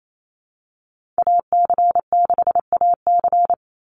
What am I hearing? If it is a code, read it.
Morse code, heard as a keyed tone, about 28 wpm, 700 Hz. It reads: AC6AC
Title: Call Signs - 28wpm